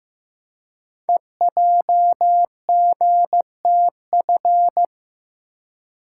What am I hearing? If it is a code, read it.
Morse code, heard as a keyed tone, about 15 wpm, 700 Hz. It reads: EJGTF
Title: Code Groups - 5 Letters - 15wpm